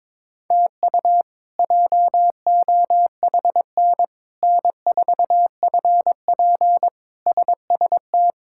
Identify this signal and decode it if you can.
Morse code, heard as a keyed tone, about 22 wpm, 700 Hz. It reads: TU JOHN N4FP SST
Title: K1USN SST QSO Practice - 22wpm